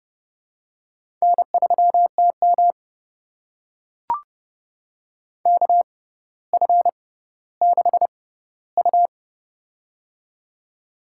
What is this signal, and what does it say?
Morse code, heard as a keyed tone, about 30 wpm, spaced as for 15 wpm, 700 Hz. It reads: N3TM KF6U
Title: USA Call Signs - 4 Characters - 6x Character Spacing - 30wpm